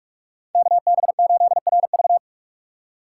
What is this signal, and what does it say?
Morse code, heard as a keyed tone, about 45 wpm, 700 Hz. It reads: KB8RV